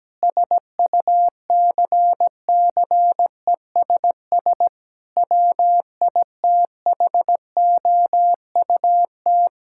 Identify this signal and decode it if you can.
Morse code, heard as a keyed tone, about 17 wpm, 700 Hz. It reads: SUCCESS WITHOUT
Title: Sentences from Top 1000 Words - Review Part 6 - 17wpm